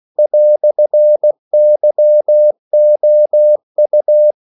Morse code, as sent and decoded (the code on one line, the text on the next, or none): .-..-. -.-- --- ..-
"YOU